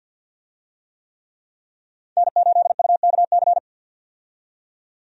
Morse code, heard as a keyed tone, about 52 wpm, 700 Hz. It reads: N8UKC
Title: USA Call Signs - 5 Characters - 50wpm